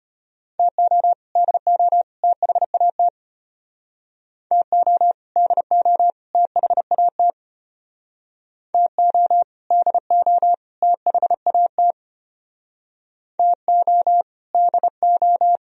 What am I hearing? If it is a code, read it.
Morse code, heard as a keyed tone, about 31 wpm, 700 Hz. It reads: TO DO THAT TO DO THAT TO DO THAT TO DO
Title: Speed-Racing - Sets of 3 Words 25wpm